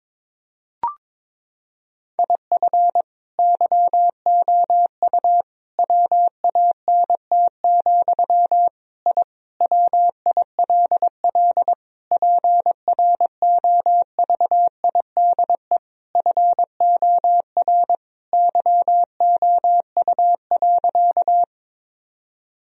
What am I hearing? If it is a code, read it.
Morse code, heard as a keyed tone, about 22 wpm, 700 Hz. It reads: IF YOU WANT, I WILL PROVIDE FOR YOU.
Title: Sentences from Top 1000 Words - Review Part 3 - 22wpm